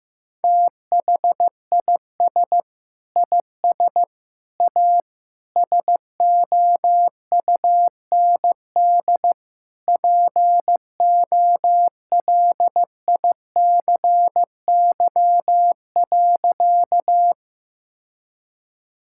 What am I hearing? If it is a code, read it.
Morse code, heard as a keyed tone, about 15 wpm, 700 Hz. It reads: THIS IS A SOUND POLICY.
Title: Sentences from Top 1000 Words - Review Part 5 - 15wpm